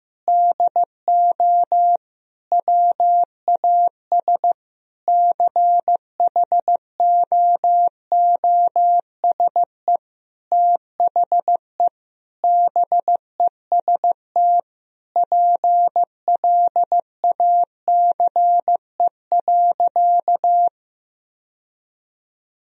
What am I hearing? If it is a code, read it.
Morse code, heard as a keyed tone, about 15 wpm, 700 Hz. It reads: DO WAS CHOOSE THE BEST PLACE.
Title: Sentences from Top 1000 Words - Review Part 4 - 15wpm